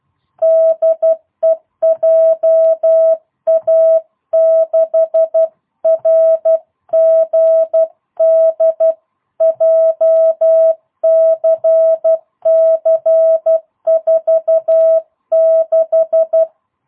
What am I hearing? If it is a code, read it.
Morse code, heard as a keyed tone, about 12 wpm, 650 Hz. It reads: DE JA6RGD JCC46
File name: jh8cbh / 2022cw